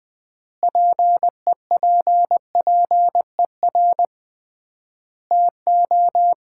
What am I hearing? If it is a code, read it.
Morse code, heard as a keyed tone, about 19 wpm, 700 Hz. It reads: PEPPER TO
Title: 3x Word-Spacing - Sets of 5 Words - 20wpm